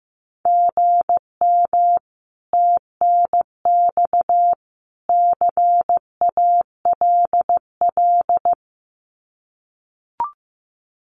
Morse code, heard as a keyed tone, about 15 wpm, 700 Hz. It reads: GM TNX CALL
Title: Common QSO Phrases 15wpm